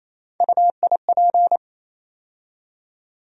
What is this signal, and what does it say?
Morse code, heard as a keyed tone, about 28 wpm, 700 Hz. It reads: UIP
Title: USA Call Sign Postfixes - 3 Letters 28wpm